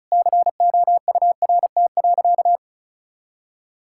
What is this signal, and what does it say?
Morse code, heard as a keyed tone, about 36 wpm, 700 Hz. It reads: COURT.
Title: Sentences from Top 1600 Words 35wpm